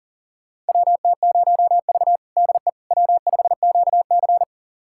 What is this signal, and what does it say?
Morse code, heard as a keyed tone, about 40 wpm, 700 Hz. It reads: WT0V DE W5QC
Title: USA Callsign DE Callsign - 4 Characters - 40wpm